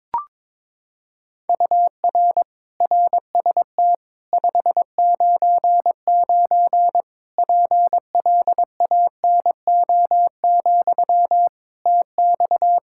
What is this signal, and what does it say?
Morse code, heard as a keyed tone, about 22 wpm, 700 Hz. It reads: UR RST 599 PLANO, TX
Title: K1USN SST QSO Practice 22wpm